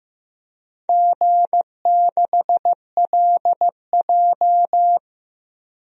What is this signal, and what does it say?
Morse code, heard as a keyed tone, about 16 wpm, 700 Hz. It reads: G6LJ
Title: DXCC-100 Call Signs - 4 Characters - Repeated 3x - 15wpm